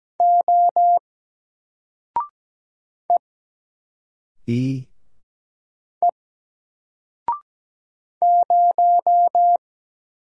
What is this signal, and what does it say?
Morse code, heard as a keyed tone, about 17 wpm, 700 Hz. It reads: O E E 0
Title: Single Letter-Number 17wpm